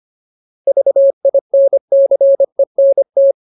Morse code, heard as a keyed tone, about 25 wpm, 550 Hz. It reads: VINCENT
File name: Ultimate Word List / 25WPM Ultimate